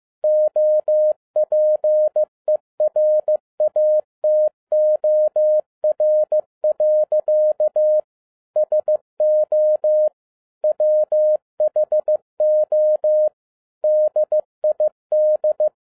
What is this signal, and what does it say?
Morse code, heard as a keyed tone, about 15 wpm, 610 Hz. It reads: OPERATOR. SO WHO DID
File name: 2011-10-mp3 / n07-A DOT 15wpm.mp3